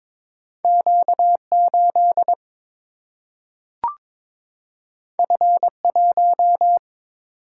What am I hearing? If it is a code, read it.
Morse code, heard as a keyed tone, about 22 wpm, 700 Hz. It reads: Q8 F1
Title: Single-Letter Single-Number 22wpm